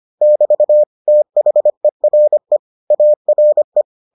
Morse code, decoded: <BT> THERE ARE

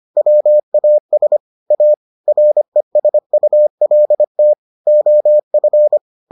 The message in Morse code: WAS A RESULT OF

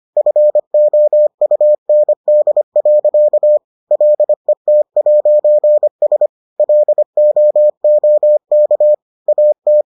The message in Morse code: FOUND. LET'S LOOK AT

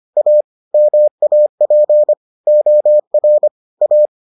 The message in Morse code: A MAP OR A

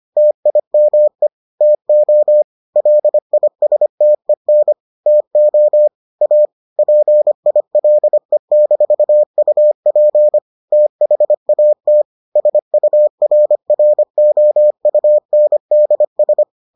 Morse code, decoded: TIME TO LISTEN TO A PILE-UP THAT SURROUNDS

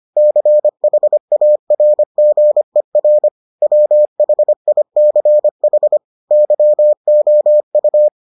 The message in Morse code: CHARGER WHICH YOU

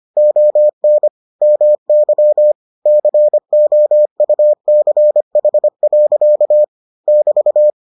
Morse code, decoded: ON MY COUCH. <BT>

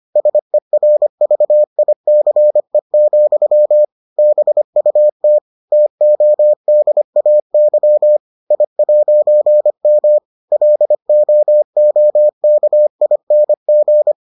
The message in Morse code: SERVICE, BUT TODAY I'M LOOKING

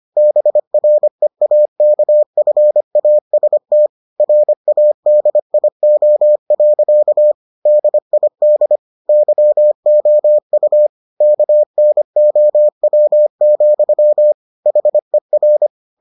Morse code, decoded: BREAKFAST RADIO. DID YOU KNOW, HER

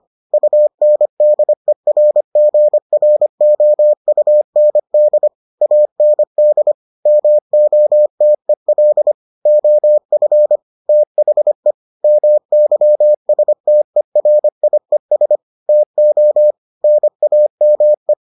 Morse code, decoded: UNDERGROUND AND MOTEL OF THE MYSTERIES TO NAME

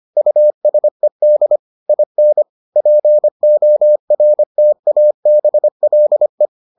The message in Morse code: USED IN PORTABLE

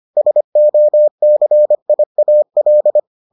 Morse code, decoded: SOCIAL

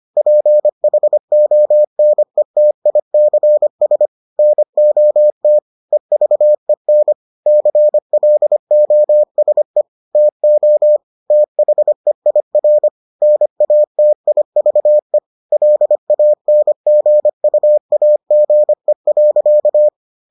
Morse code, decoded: PHONETICS NOT EVEN CLOSE TO THEIR NATIVE LANGUAGE.

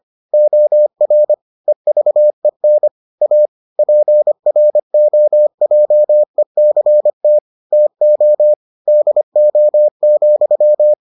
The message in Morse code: OR EVEN A PROJECT TO DO,

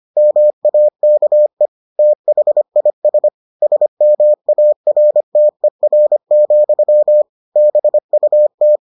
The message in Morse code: MAKE THIS SMARTER, BUT

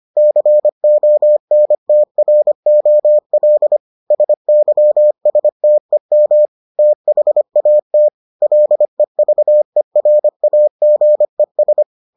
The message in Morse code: CONTROL SYSTEM THAT LEVERAGES

